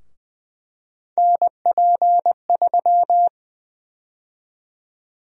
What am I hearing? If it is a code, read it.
Morse code, heard as a keyed tone, about 20 wpm, 700 Hz. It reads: NP3
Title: USA Call Sign Prefixes - 20wpm